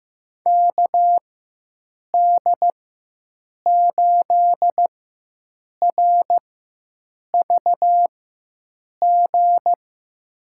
Morse code, heard as a keyed tone, about 15 wpm, spaced as for 7 wpm, 700 Hz. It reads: KD8RVG
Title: USA Call Signs - 6 Characters - 4x Character Spacing - 15wpm